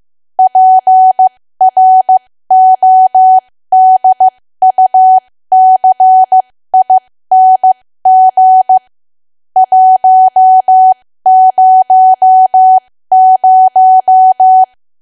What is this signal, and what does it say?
Morse code, heard as a keyed tone, about 15 wpm, spaced as for 13 wpm, 750 Hz. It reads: PRODUCING 100
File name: cwa / W1AW-Level2 / 13WPM